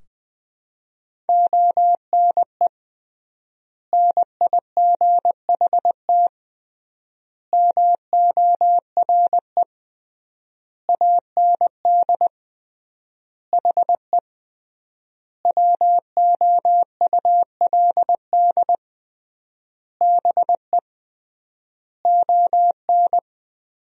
Morse code, heard as a keyed tone, about 21 wpm, 700 Hz. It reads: ONE NIGHT MORE AND HE WOULD BE ON
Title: 3x Word-Spacing - Sentences from Top 300 Words - 20wpm